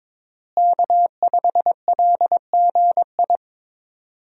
Morse code, heard as a keyed tone, about 22 wpm, 700 Hz. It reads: K5LGI
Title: USA Call Signs - 5 Characters - Repeated 3x - 22wpm